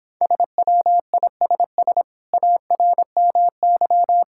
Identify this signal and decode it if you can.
Morse code, heard as a keyed tone, about 26 wpm, 700 Hz. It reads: SWISS ARMY